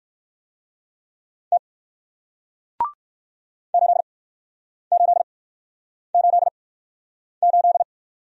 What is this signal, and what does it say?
Morse code, heard as a keyed tone, about 54 wpm, spaced as for 9 wpm, 700 Hz. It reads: T 8888
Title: Speed-Racing Single Letter-Number 45wpm